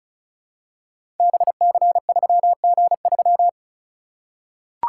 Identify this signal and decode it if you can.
Morse code, heard as a keyed tone, about 36 wpm, 700 Hz. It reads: BC3G3
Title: Code Groups - 5 Letter:Numbers - 35wpm